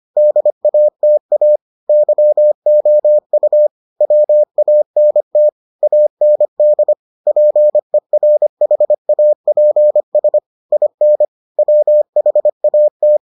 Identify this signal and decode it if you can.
Morse code, heard as a keyed tone, about 25 wpm, 600 Hz. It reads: DATA YOU WANT AND PERHAPS IN WHAT